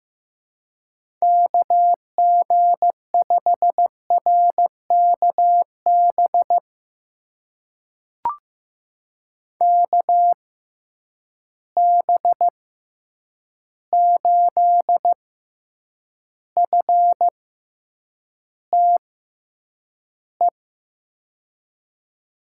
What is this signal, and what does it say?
Morse code, heard as a keyed tone, about 15 wpm, spaced as for 8 wpm, 700 Hz. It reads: KG5RKB KB8FTE
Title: USA Call Signs - 6 Characters - 6x Character Spacing - 15wpm